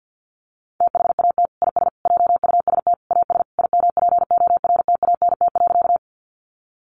Morse code, heard as a keyed tone, about 48 wpm, 700 Hz. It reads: THAT IS JUST AS IMPORTANT.
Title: Sentences from Top 600 Words 50wpm